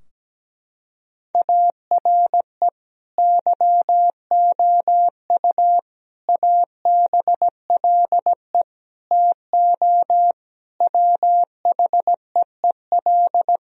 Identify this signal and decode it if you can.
Morse code, heard as a keyed tone, about 17 wpm, 700 Hz. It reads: ARE YOU ABLE TO WHEEL
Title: Sentences from Top 1000 Words - Review Part 3 - 17wpm